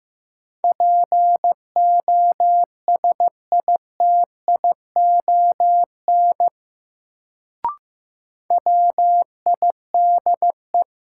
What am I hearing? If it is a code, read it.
Morse code, heard as a keyed tone, about 15 wpm, 700 Hz. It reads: POSITION WIDE
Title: Sentences from Top 900 Words - New Word Drill - 15wpm